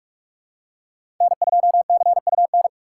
Morse code, heard as a keyed tone, about 47 wpm, 700 Hz. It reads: N2CUN
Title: USA Call Signs - 5 Characters - 6x Character Spacing - 45wpm